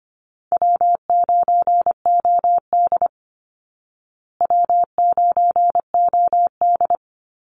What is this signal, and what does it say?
Morse code, heard as a keyed tone, about 25 wpm, 700 Hz. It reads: W9OD W9OD
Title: Call Signs Repeated 3x - 25wpm